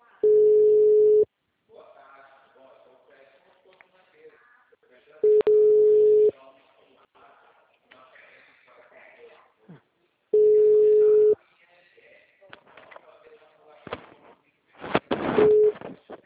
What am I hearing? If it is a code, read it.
Ringback tone (Europe, 425 Hz), with speech.